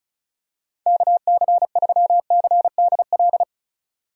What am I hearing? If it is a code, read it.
Morse code, heard as a keyed tone, about 34 wpm, 700 Hz. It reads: KC3CDL
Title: USA Call Signs - 6 Characters - 35wpm